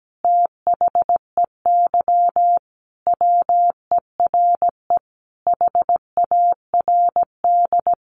Morse code, heard as a keyed tone, about 17 wpm, 700 Hz. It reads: THEY WERE HARD
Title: Sentences from Top 1300 Words 17wpm